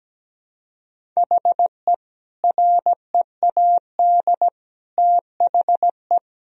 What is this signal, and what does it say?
Morse code, heard as a keyed tone, about 17 wpm, 700 Hz. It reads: HE READ THE
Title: Sentences from Top 1000 Words - Review Part 1 - 17wpm